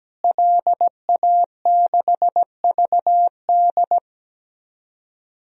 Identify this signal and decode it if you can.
Morse code, heard as a keyed tone, about 17 wpm, 700 Hz. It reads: LA6VD
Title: DXCC-100 Call Signs - 5 Characters - Repeated 3x - 17wpm